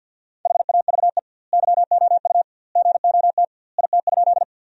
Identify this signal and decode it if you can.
Morse code, heard as a keyed tone, about 47 wpm, 700 Hz. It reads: HAVE YOU GOT IT?
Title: 2x Word-Spacing - Sentences from Top 300 Words - 50wpm